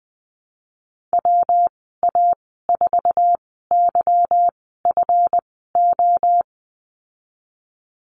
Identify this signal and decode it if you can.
Morse code, heard as a keyed tone, about 20 wpm, spaced as for 14 wpm, 700 Hz. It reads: WA4YFO
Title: 193 Lesson 31g - Letter v - Call Signs 20wpm